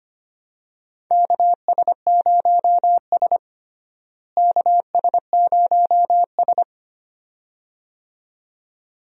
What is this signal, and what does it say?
Morse code, heard as a keyed tone, about 25 wpm, 700 Hz. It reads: KS0S KS0S
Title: USA Call Signs - 4 Characters - Repeated 2x - 25wpm